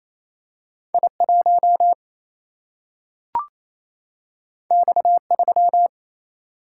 Morse code, heard as a keyed tone, about 28 wpm, 700 Hz. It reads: I 1 X 3
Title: Single-Letter Single-Number 28wpm